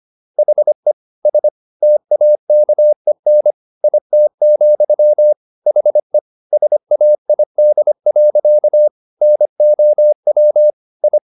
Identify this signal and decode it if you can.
Morse code, heard as a keyed tone, about 25 wpm, 600 Hz. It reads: HE S TAKEN IT, HE SAID. NOW I
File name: the old man and the sea part 1 25wpm